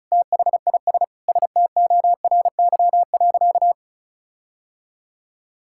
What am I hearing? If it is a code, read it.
Morse code, heard as a keyed tone, about 36 wpm, 700 Hz. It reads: THIS STORY.
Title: Sentences from Top 1600 Words 35wpm